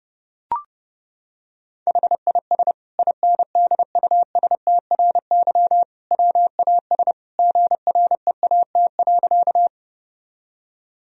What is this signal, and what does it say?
Morse code, heard as a keyed tone, about 30 wpm, 700 Hz. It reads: HIS INDUSTRY WAS GREAT.
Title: Sentences from Top 1000 Words - Review Part 4 - 30wpm